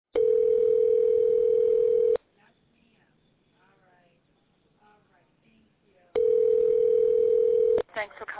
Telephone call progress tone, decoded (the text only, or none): ringback tone